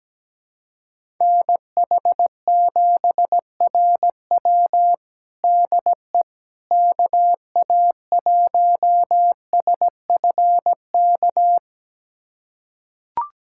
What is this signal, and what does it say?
Morse code, heard as a keyed tone, about 17 wpm, 700 Hz. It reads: NH7RW DE KA1SFK
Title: USA Callsign DE Callsign 17wpm